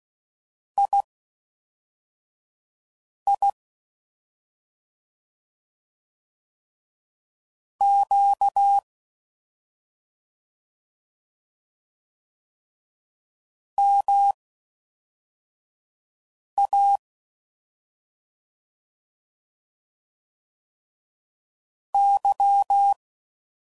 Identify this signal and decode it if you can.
Morse code, heard as a keyed tone, about 16 wpm, spaced as for 4 wpm, 800 Hz. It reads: II Q MA Y
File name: Mixed / CW Comparisons / CW Comparisons Broken Characters